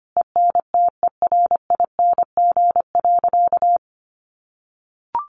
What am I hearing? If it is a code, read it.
Morse code, heard as a keyed tone, about 25 wpm, 700 Hz. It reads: ENTERING.
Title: A Sound Of Thunder 25wpm